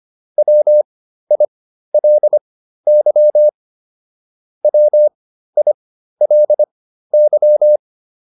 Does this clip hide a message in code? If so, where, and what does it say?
Yes: Morse code from 0.4 to 7.8 s, heard as a keyed tone, about 25 wpm, spaced as for 13 wpm, 600 Hz. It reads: WILY WILY